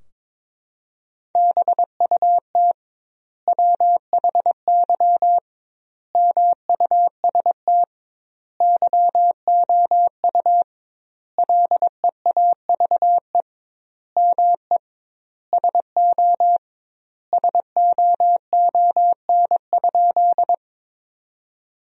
Morse code, heard as a keyed tone, about 21 wpm, 700 Hz. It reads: BUT WHY MUST YOU LEAVE ME SO SOON?
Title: 2x Word-Spacing - Sentences from Top 400 Words - 22wpm